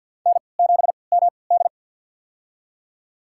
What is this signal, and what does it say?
Morse code, heard as a keyed tone, about 46 wpm, spaced as for 27 wpm, 700 Hz. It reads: N7MD
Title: USA Call Signs - 4 Characters - 3x Character Spacing - 50wpm